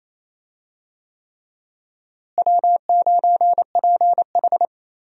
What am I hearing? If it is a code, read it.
Morse code, heard as a keyed tone, about 28 wpm, 700 Hz. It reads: W9PH
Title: USA Call Signs - 4 Characters - Repeated 3x - 28wpm